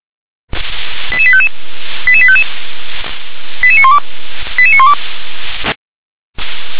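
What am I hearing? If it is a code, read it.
Five-tone selective call (ZVEI), heard as quick runs of five tones, tones of about 70 ms. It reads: 88855 88855 88912 88912
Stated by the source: ZVEI five-tone selective call received by SDR.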